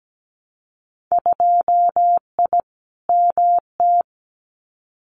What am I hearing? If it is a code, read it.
Morse code, heard as a keyed tone, about 17 wpm, 700 Hz. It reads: 2I MT
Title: Winter Field Day Exchanges Encore 17wpm